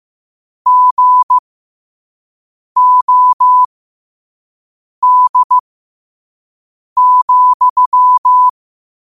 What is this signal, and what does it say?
Morse code, heard as a keyed tone, about 15 wpm, spaced as for 5 wpm, 1000 Hz. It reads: GOD,
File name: qotdfiles / 5wpm